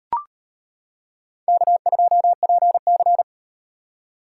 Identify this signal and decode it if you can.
Morse code, heard as a keyed tone, about 38 wpm, 700 Hz. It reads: K2PC